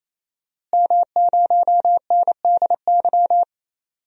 Morse code, heard as a keyed tone, about 28 wpm, 700 Hz. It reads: M0NDY
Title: DXCC-100 Call Signs - 5 Characters - Repeated 3x - 28wpm